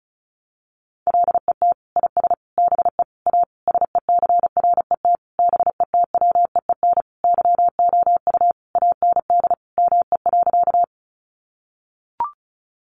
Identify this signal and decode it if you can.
Morse code, heard as a keyed tone, about 34 wpm, 700 Hz. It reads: LET IS BE A SECRET BETWEEN YOU AND ME.
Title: Sentences from Top 1200 Words 35wpm